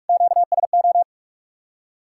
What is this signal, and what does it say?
Morse code, heard as a keyed tone, about 57 wpm, 700 Hz. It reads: QSO